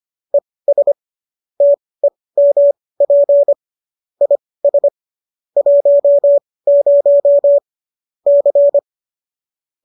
Morse code, heard as a keyed tone, about 25 wpm, spaced as for 18 wpm, 570 Hz. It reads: ES TEMP IS 10 C